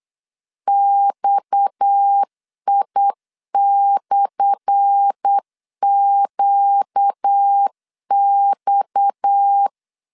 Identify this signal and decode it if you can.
Morse code, heard as a keyed tone, about 8 wpm, 800 Hz. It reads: XI/QX
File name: LEZIONI / 10 SIGNS / 10A SLASH 8wpm 3452